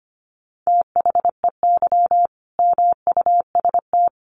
Morse code, heard as a keyed tone, about 25 wpm, 700 Hz. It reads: THEY MUST
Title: Sentences from Top 1400 Words 25wpm